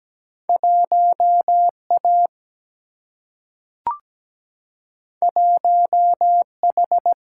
Morse code, heard as a keyed tone, about 17 wpm, 700 Hz. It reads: 1A 1H
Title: Single-Letter Single-Number 17wpm